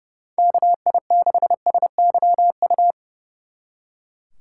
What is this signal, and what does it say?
Morse code, heard as a keyed tone, about 38 wpm, 700 Hz. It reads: KI6SYU